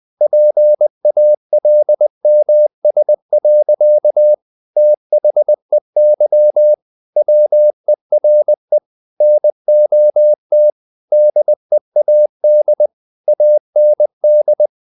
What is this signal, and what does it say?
Morse code, heard as a keyed tone, about 20 wpm, 600 Hz. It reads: PALMS. THEY WERE NOT DEAD AND
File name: the old man and the sea part 2 20wpm